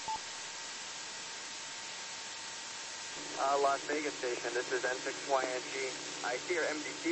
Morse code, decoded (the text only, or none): none